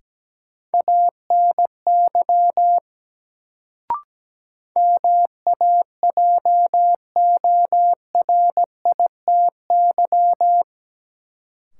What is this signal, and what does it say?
Morse code, heard as a keyed tone, about 17 wpm, 700 Hz. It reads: ANY MAJORITY
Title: Sets of 3 Words - New Word Drill - 17wpm